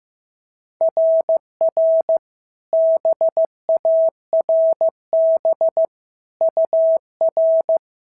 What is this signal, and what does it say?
Morse code, heard as a keyed tone, about 14 wpm, 650 Hz. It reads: RR BARB UR
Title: SOTA QSO Practice 15wpm